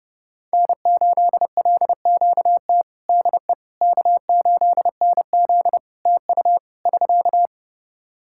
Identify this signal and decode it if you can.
Morse code, heard as a keyed tone, about 30 wpm, 700 Hz. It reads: N8LQT DE K8NZ TU <SK>
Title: Practice QSO - Exchange 4 - Ending - 30wpm